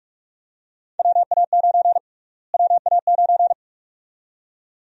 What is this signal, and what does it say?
Morse code, heard as a keyed tone, about 46 wpm, 700 Hz. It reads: WA9 WA9
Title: USA Call Sign Prefixes - Repeated 2x - 45wpm